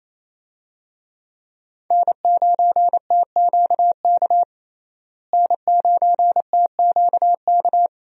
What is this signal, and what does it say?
Morse code, heard as a keyed tone, about 28 wpm, 700 Hz. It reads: N9TQK N9TQK
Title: USA Call Signs - 5 Characters - Repeated 3x - 28wpm